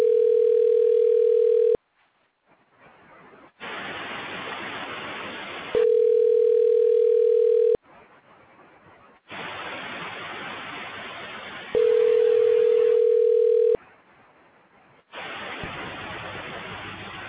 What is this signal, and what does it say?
Ringback tone (North America), 440 + 480 Hz.